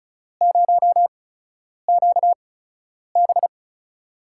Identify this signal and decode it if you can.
Morse code, heard as a keyed tone, about 34 wpm, spaced as for 10 wpm, 700 Hz. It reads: 0QB
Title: USA Call Signs - 5 Characters - 8x Character Spacing - 35wpm